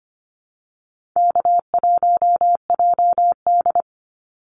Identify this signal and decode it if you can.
Morse code, heard as a keyed tone, about 25 wpm, 700 Hz. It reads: K1JD
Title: NA SOTA Activator Callsigns 25wpm